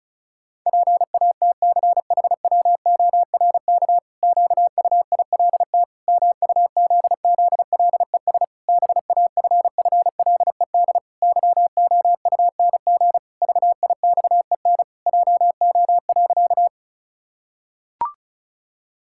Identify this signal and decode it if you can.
Morse code, heard as a keyed tone, about 36 wpm, 700 Hz. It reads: PATCHWORK QUILT MUZZLES BAFFLED YOUNG VIXEN JO.
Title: Pangram Challenge v2 - 35wpm